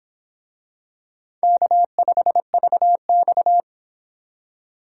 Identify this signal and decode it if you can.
Morse code, heard as a keyed tone, about 25 wpm, 700 Hz. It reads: K5VX